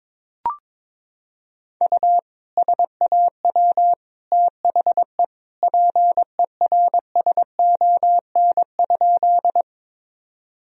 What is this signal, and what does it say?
Morse code, heard as a keyed tone, about 22 wpm, 700 Hz. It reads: U SAW THE PERSON?
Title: Sentences from Top 400 Words and Abbreviations 22wpm